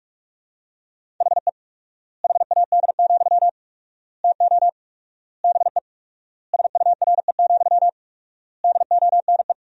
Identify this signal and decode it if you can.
Morse code, heard as a keyed tone, about 42 wpm, 700 Hz. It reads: HE HAD, TO BE SURE, DONE
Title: 4x Word-Spacing - Sentences from Top 500 Words - 45wpm